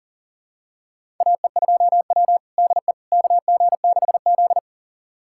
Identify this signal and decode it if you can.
Morse code, heard as a keyed tone, about 40 wpm, 700 Hz. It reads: AE2W DE KG6Z
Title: USA Callsign DE Callsign - 4 Characters - 40wpm